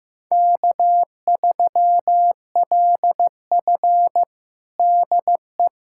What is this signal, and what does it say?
Morse code, heard as a keyed tone, about 15 wpm, 700 Hz. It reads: K3LF DE
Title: USA Callsign DE Callsign - 4 Characters - 15wpm